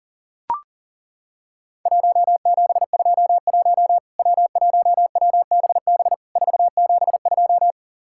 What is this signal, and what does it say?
Morse code, heard as a keyed tone, about 40 wpm, 700 Hz. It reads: 1721 W1WBB 472
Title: Worked All Europe DX Contest - QTC Exchange - 40wpm